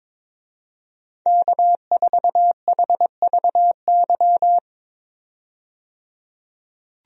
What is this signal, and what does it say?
Morse code, heard as a keyed tone, about 22 wpm, 700 Hz. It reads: K4HVY
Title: USA Call Signs - 5 Characters - 22wpm